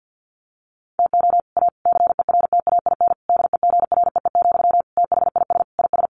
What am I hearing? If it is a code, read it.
Morse code, heard as a keyed tone, about 48 wpm, 700 Hz. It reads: TO A CERTAIN DEGREE, THIS IS
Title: Sentences from Top 1500 Words 50wpm